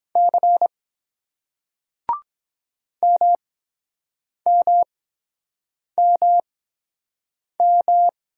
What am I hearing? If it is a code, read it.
Morse code, heard as a keyed tone, about 23 wpm, spaced as for 7 wpm, 700 Hz. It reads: C MMMM